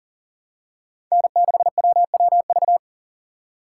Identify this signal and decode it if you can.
Morse code, heard as a keyed tone, about 40 wpm, 700 Hz. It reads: N6WWV
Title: USA Call Signs - 5 Characters - 3x Character Spacing - 40wpm